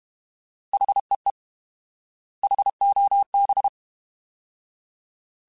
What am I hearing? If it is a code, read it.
Morse code, heard as a keyed tone, about 32 wpm, 800 Hz. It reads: HEE HOB